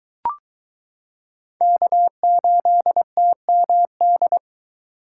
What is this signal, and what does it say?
Morse code, heard as a keyed tone, about 22 wpm, 700 Hz. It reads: K8TMD